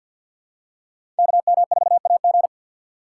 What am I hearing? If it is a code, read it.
Morse code, heard as a keyed tone, about 53 wpm, 700 Hz. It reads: KM4AG
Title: USA Call Signs - 2x Character Spacing - 50wpm